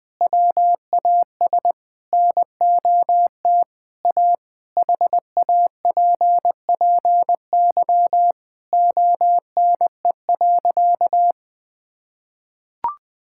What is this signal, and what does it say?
Morse code, heard as a keyed tone, about 21 wpm, 700 Hz. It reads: WAS NOT A HAPPY ONE.
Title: Sentences from Top 1000 Words - Review Part 5 - 20wpm